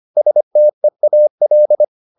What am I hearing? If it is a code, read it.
Morse code, heard as a keyed tone, about 25 wpm, 600 Hz. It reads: STEAL
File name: the old man and the sea part 1 25wpm